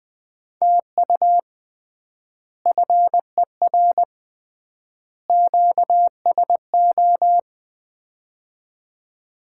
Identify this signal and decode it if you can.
Morse code, heard as a keyed tone, about 20 wpm, 700 Hz. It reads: TU FER QSO
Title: Common QSO Phrases - 3x Word-Spacing - 20wpm